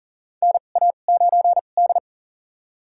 Morse code, heard as a keyed tone, about 37 wpm, spaced as for 28 wpm, 700 Hz. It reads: NA9D
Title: USA Call Signs - 4 Characters - 2x Character Spacing - 40wpm